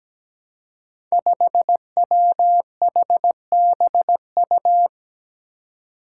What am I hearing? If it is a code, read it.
Morse code, heard as a keyed tone, about 17 wpm, 700 Hz. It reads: 5WHBU
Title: Five Challenging Letter-Number Combinations 17wpm